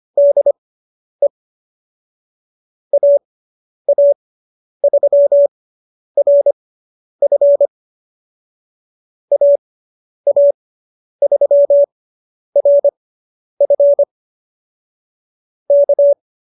Morse code, heard as a keyed tone, about 25 wpm, spaced as for 10 wpm, 590 Hz. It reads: DE AA3RF AA3RF K